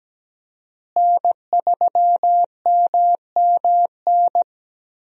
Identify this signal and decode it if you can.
Morse code, heard as a keyed tone, about 17 wpm, 700 Hz. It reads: N3MMN
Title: USA Call Signs - 5 Characters - Repeated 3x - 17wpm